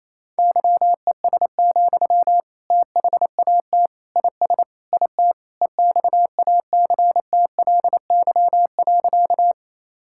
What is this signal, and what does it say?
Morse code, heard as a keyed tone, about 28 wpm, 700 Hz. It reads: YES, THAT IS IT EXACTLY.
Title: Sentences from Top 1000 Words - Review Part 6 - 28wpm